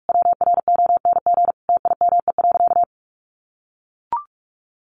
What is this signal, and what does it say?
Morse code, heard as a keyed tone, about 44 wpm, 700 Hz. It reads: WRONG TIME.
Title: Sentences from Top 900 Words 45wpm